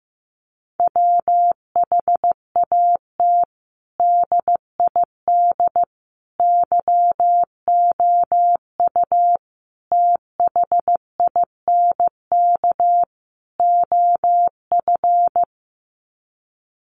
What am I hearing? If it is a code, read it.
Morse code, heard as a keyed tone, about 15 wpm, 700 Hz. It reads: WHAT DID YOU THINK OF
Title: Sets of 5 Words Encore2 15wpm